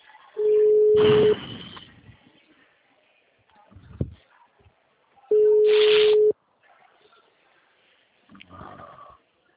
Ringback tone (Europe, 425 Hz).